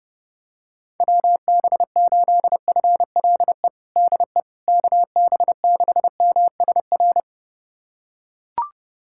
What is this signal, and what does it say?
Morse code, heard as a keyed tone, about 30 wpm, 700 Hz. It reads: WB8FLE DE KB6MSR
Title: USA Callsign DE Callsign - 6 Characters - 30wpm